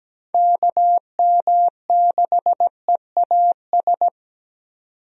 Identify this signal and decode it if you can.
Morse code, heard as a keyed tone, about 17 wpm, 700 Hz. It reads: KM6EAS
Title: USA Call Signs - 6 Characters - 17wpm